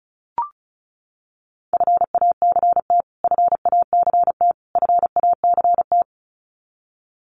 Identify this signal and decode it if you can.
Morse code, heard as a keyed tone, about 34 wpm, 700 Hz. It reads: FACT FACT FACT
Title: Top 500 Words Repeated 35wpm